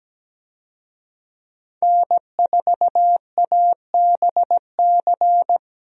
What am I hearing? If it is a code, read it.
Morse code, heard as a keyed tone, about 17 wpm, 700 Hz. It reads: N4ABC
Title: USA Call Signs - 5 Characters - Repeated 3x - 17wpm